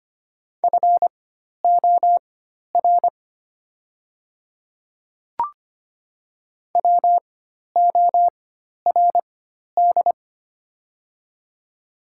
Morse code, heard as a keyed tone, about 25 wpm, spaced as for 12 wpm, 700 Hz. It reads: FOR WORD
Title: 4x Character Spacing - Top 100 Words - 25wpm